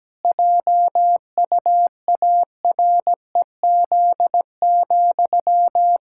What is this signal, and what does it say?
Morse code, heard as a keyed tone, about 17 wpm, 700 Hz. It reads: JUAREZ,
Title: Pangram Challenge v2 - 17wpm